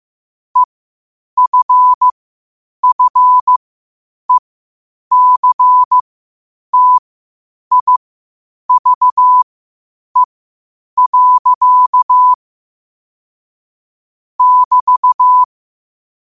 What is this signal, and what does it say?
Morse code, heard as a keyed tone, about 15 wpm, spaced as for 8 wpm, 1000 Hz. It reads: EFFECTIVE. <BT>
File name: qotdfiles / 8wpm